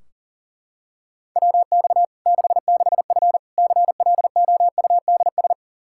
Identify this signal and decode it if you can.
Morse code, heard as a keyed tone, about 37 wpm, 700 Hz. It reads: WX 66F CLOUDS
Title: Practice QSO - Exchange 2 - Info - 40wpm